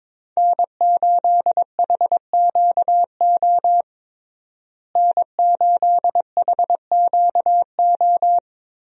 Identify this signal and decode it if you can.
Morse code, heard as a keyed tone, about 22 wpm, 700 Hz. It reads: N8HQO N8HQO
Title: USA Call Signs - 5 Characters - Repeated 3x - 22wpm